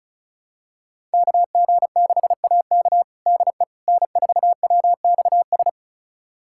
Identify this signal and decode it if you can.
Morse code, heard as a keyed tone, about 36 wpm, 690 Hz. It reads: KG6AK DE N4WXS
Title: USA Callsign DE Callsign - 5 Characters - 35wpm